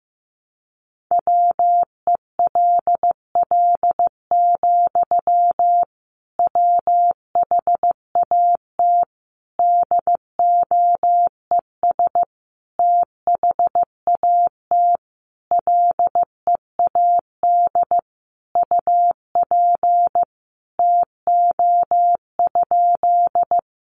Morse code, heard as a keyed tone, about 15 wpm, 700 Hz. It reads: WELL, WHAT DOES THAT LEAD UP TO?
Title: Sentences from Top 500 Words 15wpm